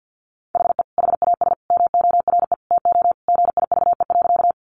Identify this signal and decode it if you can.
Morse code, heard as a keyed tone, about 50 wpm, 700 Hz. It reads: HE HAS MORE TO GIVE.
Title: Sentences from Top 200 Words 50wpm